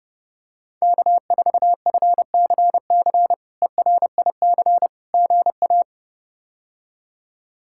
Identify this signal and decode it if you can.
Morse code, heard as a keyed tone, about 30 wpm, 700 Hz. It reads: K4FCC ERIC GA
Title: K1USN SST QSO Practice - 30wpm